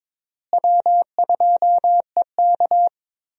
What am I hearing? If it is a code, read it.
Morse code, heard as a keyed tone, about 22 wpm, 700 Hz. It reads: W2EK
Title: USA Call Signs - 4 Characters - Repeated 2x - 22wpm